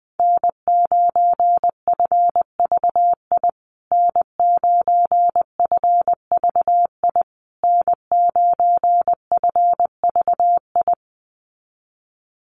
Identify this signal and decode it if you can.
Morse code, heard as a keyed tone, about 19 wpm, 700 Hz. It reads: N9FVI N9FVI N9FVI
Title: USA Call Signs Repeated 3x 20wpm